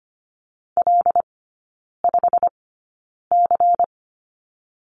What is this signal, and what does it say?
Morse code, heard as a keyed tone, about 25 wpm, spaced as for 9 wpm, 700 Hz. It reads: L5C
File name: course / 25:10wpm / 093 Lesson 16e - Letter c - Three Characters 25wpm